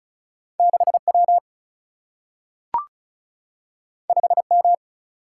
Morse code, heard as a keyed tone, about 36 wpm, 700 Hz. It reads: 6 W 5 M
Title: Single-Letter Single-Number 35wpm